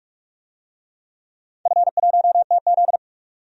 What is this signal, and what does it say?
Morse code, heard as a keyed tone, about 42 wpm, 700 Hz. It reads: F1TZ
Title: DXCC-100 Call Signs - 4 Characters - 45wpm